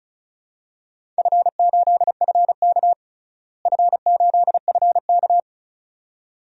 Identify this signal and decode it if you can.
Morse code, heard as a keyed tone, about 36 wpm, 700 Hz. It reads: F8FK F8FK
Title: DXCC-100 Call Signs - 4 Characters - Repeated 2x - 35wpm